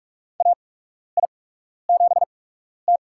Morse code, heard as a keyed tone, about 46 wpm, spaced as for 12 wpm, 700 Hz. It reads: AI7T
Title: USA Call Signs - 4 Characters - 8x Character Spacing - 45wpm